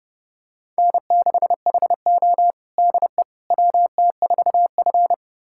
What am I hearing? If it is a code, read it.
Morse code, heard as a keyed tone, about 30 wpm, 700 Hz. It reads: N6HO DE WT4F
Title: USA Callsign DE Callsign - 4 Characters - 30wpm